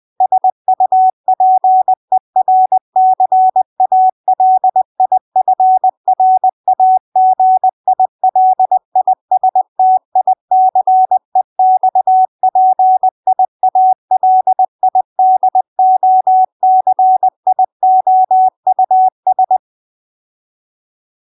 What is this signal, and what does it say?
Morse code, heard as a keyed tone, about 20 wpm, 750 Hz. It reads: SUPERCALIFRAGILISTICEXPIALIDOCIOUS